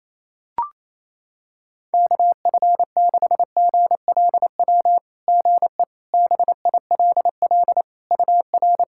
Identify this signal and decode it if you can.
Morse code, heard as a keyed tone, about 28 wpm, 660 Hz. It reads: KF6GLW GE BILL UR
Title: SOTA QSO Practice 28wpm